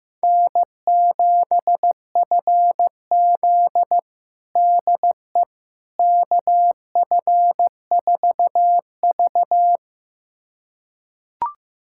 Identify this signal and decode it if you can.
Morse code, heard as a keyed tone, about 15 wpm, 700 Hz. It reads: N7FZ DE KF4V
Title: USA Callsign DE Callsign - 4 Characters - 15wpm